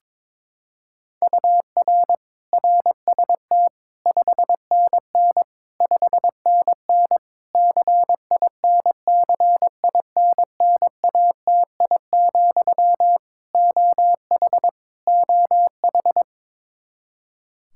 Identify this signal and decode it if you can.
Morse code, heard as a keyed tone, about 22 wpm, 700 Hz. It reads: UR RST 5NN 5NN CINCINNATI, OH OH
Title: Practice QSO - Exchange 1 - Beginning - 22wpm